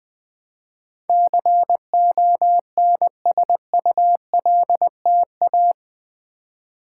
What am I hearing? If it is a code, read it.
Morse code, heard as a keyed tone, about 20 wpm, 700 Hz. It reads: CONSULTA
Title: Build A Word - 4 Letters At A Time - 12-Letter Words - 20wpm